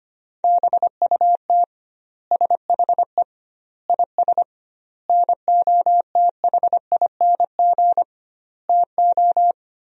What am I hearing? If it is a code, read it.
Morse code, heard as a keyed tone, about 25 wpm, 700 Hz. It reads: BUT SHE IS NOTHING TO
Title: 2x Word-Spacing - Sentences from Top 500 Words - 25wpm